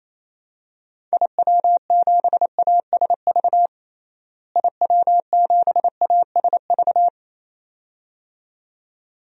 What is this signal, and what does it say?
Morse code, heard as a keyed tone, about 28 wpm, 700 Hz. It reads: IW7ASV IW7ASV
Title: DXCC-100 Call Signs - 6 Characters - Repeated 2x - 28wpm